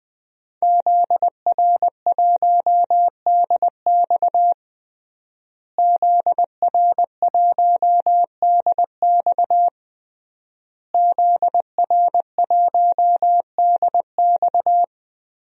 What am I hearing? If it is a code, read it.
Morse code, heard as a keyed tone, about 20 wpm, 700 Hz. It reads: ZR1DX ZR1DX ZR1DX
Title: DXCC-100 Call Signs - 5 Characters - Repeated 3x - 20wpm